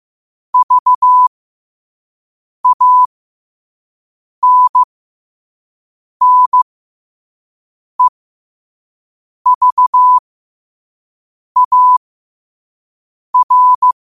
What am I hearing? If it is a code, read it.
Morse code, heard as a keyed tone, about 15 wpm, spaced as for 5 wpm, 1000 Hz. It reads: VANNEVAR
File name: qotdfiles / 5wpm